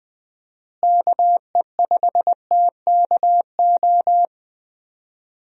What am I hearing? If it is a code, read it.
Morse code, heard as a keyed tone, about 20 wpm, 700 Hz. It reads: KE5TKO
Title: USA Call Signs - 6 Characters - Repeated 2x - 20wpm